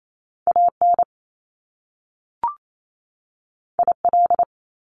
Morse code, heard as a keyed tone, about 28 wpm, 700 Hz. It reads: A N I L
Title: Top Two Letter Combinations in Words 28wpm